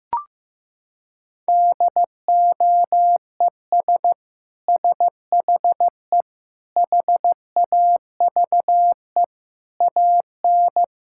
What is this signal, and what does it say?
Morse code, heard as a keyed tone, about 16 wpm, 700 Hz. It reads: DOES SHE HAVE AN
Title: Sentences from Top 1000 Words - Review Part 6 - 15wpm